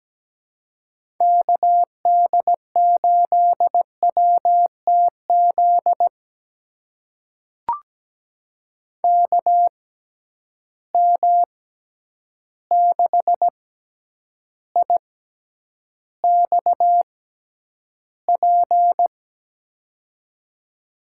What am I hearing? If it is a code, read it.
Morse code, heard as a keyed tone, about 17 wpm, spaced as for 9 wpm, 700 Hz. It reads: KD8WTZ KM6IXP
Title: USA Call Signs - 6 Characters - 6x Character Spacing - 17wpm